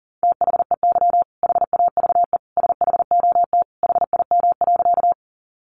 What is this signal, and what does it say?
Morse code, heard as a keyed tone, about 37 wpm, 700 Hz. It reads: THEY HAVE SHOT HIM.
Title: Sentences from Top 1100 Words 40wpm